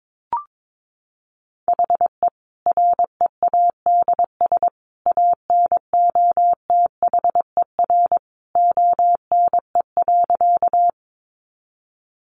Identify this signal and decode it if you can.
Morse code, heard as a keyed tone, about 22 wpm, 700 Hz. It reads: HE READS ANOTHER ONE.
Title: Additional Easy Phrases 22wpm